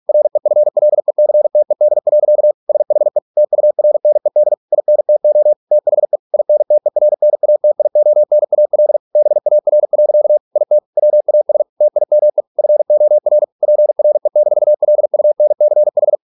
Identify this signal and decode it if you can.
Morse code, heard as a keyed tone, about 46 wpm, 600 Hz. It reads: REFLECTED. SHE TURNED INTO THE INTERNATIONAL BAR. IT WAS TIME FOR PRE-LUNCH